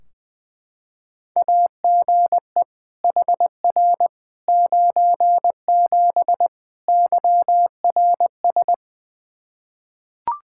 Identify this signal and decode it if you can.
Morse code, heard as a keyed tone, about 20 wpm, 700 Hz. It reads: AGE HR 97 YRS
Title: Practice QSO - Exchange 3 - More Info - 20wpm